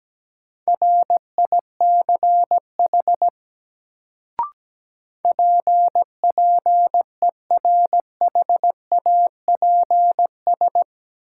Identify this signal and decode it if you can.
Morse code, heard as a keyed tone, about 17 wpm, 700 Hz. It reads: RICH PPERHAPS
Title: Sentences from Top 1100 Words - New Word Drill Alt - 17wpm